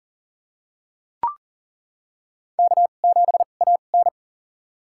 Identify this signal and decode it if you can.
Morse code, heard as a keyed tone, about 40 wpm, spaced as for 29 wpm, 700 Hz. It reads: K7AN
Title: USA Call Signs - 2x Character Spacing - 40wpm